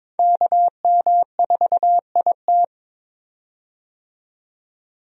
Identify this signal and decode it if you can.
Morse code, heard as a keyed tone, about 22 wpm, 700 Hz. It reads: KM4IT
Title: USA Call Signs - 5 Characters - 22wpm